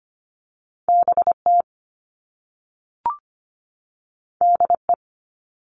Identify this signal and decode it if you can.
Morse code, heard as a keyed tone, about 25 wpm, 700 Hz. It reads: BT DE
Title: All Cut Numbers - 2-Digit 25wpm